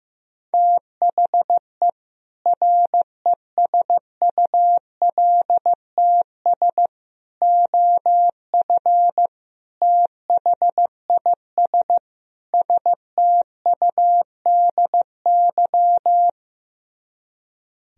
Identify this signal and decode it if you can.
Morse code, heard as a keyed tone, about 15 wpm, 700 Hz. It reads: THE RESULTS OF THIS STUDY
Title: Speed-Racing - Sets of 5 Words 15wpm